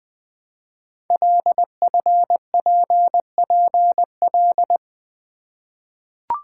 Morse code, heard as a keyed tone, about 21 wpm, 700 Hz. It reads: LFPPL
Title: Code Groups - 5 Letters - 20wpm